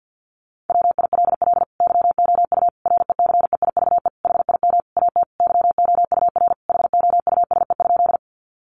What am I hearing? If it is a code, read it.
Morse code, heard as a keyed tone, about 50 wpm, 700 Hz. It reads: WILL YOU RECEIVE HIM AT YOUR HOUSE?
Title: Sentences from Top 1000 Words 50wpm